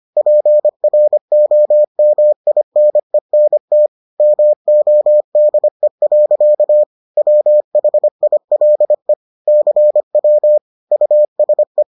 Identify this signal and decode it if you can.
Morse code, heard as a keyed tone, about 25 wpm, 600 Hz. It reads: PROMINENT MODE. WHILE CW USE